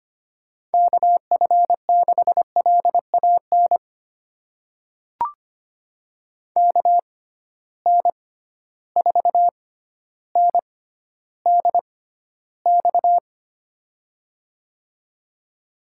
Morse code, heard as a keyed tone, about 25 wpm, spaced as for 13 wpm, 700 Hz. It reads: KF6LAN KN4NDX
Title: USA Call Signs - 6 Characters - 6x Character Spacing - 25wpm